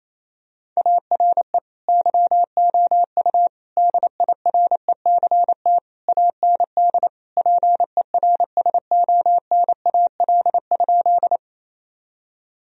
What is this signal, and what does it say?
Morse code, heard as a keyed tone, about 27 wpm, 700 Hz. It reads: ARE YOU DIRECT AND PERSONAL?
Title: Sentences from Top 1000 Words - Review Part 4 - 28wpm